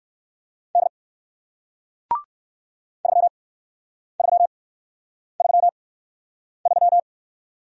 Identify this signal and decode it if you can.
Morse code, heard as a keyed tone, about 54 wpm, spaced as for 9 wpm, 700 Hz. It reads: D 3333
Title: Speed-Racing Single Letter-Number 45wpm